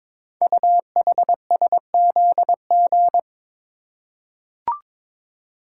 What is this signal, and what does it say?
Morse code, heard as a keyed tone, about 22 wpm, 700 Hz. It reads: UHSZG
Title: Code Groups - 5 Letters - 22wpm